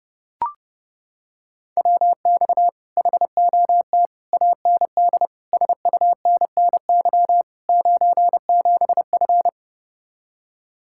Morse code, heard as a keyed tone, about 30 wpm, 700 Hz. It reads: WX HOT AND SUNNY 97F